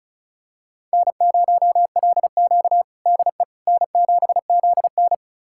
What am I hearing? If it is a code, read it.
Morse code, heard as a keyed tone, about 36 wpm, 700 Hz. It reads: N0LQ DE N7ZN
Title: USA Callsign DE Callsign - 4 Characters - 35wpm